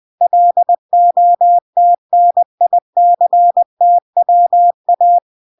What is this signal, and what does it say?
Morse code, heard as a keyed tone, about 20 wpm, 700 Hz. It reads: LOTNICTWA
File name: recordings / 20wpm0000